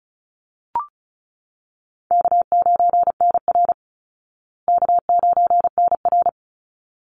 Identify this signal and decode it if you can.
Morse code, heard as a keyed tone, about 34 wpm, 700 Hz. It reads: K9NR K9NR
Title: Easy Call Signs Repeated 2x - 35wpm